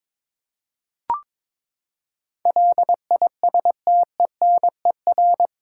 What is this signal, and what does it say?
Morse code, heard as a keyed tone, about 21 wpm, 700 Hz. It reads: LISTENER
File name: practice / 22wpm / Eight Letter Words - 22wpm